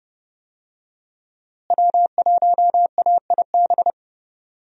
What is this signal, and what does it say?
Morse code, heard as a keyed tone, about 30 wpm, 700 Hz. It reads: W1AIB
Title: USA Call Signs - 5 Characters - Repeated 2x - 30wpm